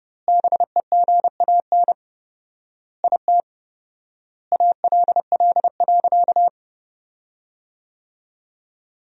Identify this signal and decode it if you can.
Morse code, heard as a keyed tone, about 32 wpm, 700 Hz. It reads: BEGAN IT ALL.
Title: 4x Word-Spacing - Sentences from Top 300 Words - 30wpm